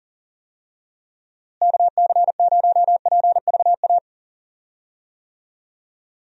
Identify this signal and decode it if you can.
Morse code, heard as a keyed tone, about 40 wpm, 700 Hz. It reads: KC0PVA
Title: USA Call Signs - 6 Characters - Repeated 2x - 40wpm